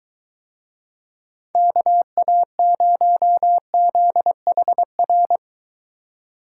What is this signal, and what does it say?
Morse code, heard as a keyed tone, about 22 wpm, 700 Hz. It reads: KA0ZHR